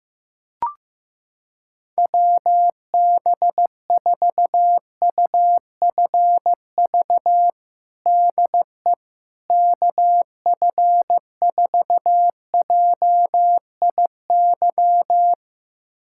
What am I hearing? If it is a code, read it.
Morse code, heard as a keyed tone, about 15 wpm, 700 Hz. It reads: WB4UFV DE KF4JIY
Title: USA Callsign DE Callsign - 6 Characters - 15wpm